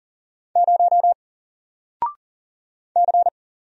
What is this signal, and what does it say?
Morse code, heard as a keyed tone, about 40 wpm, 700 Hz. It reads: 0 C
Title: Single Letter-Number 40wpm